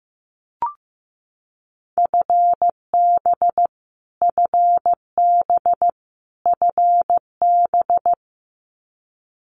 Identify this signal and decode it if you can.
Morse code, heard as a keyed tone, about 15 wpm, 700 Hz. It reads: FB FB FB
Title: Abbreviations Repeated 3x - 15wpm